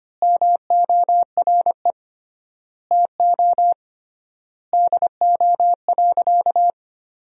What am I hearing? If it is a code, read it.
Morse code, heard as a keyed tone, about 25 wpm, 700 Hz. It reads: MORE TO DO.
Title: 3x Word-Spacing - Sentences from Top 100 Words - 25wpm